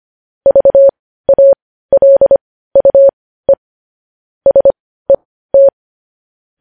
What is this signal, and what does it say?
Morse code, heard as a keyed tone, about 25 wpm, spaced as for 15 wpm, 550 Hz. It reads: VALUE SET
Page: News Headlines in Morse Code at 15 WPM